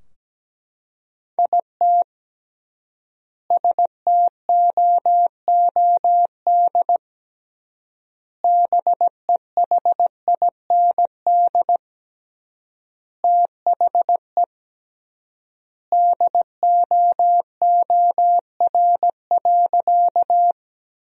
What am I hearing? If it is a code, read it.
Morse code, heard as a keyed tone, about 17 wpm, 700 Hz. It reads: IT STOOD BEHIND THE DOOR.
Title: 3x Word-Spacing - Sentences from Top 500 Words - 17wpm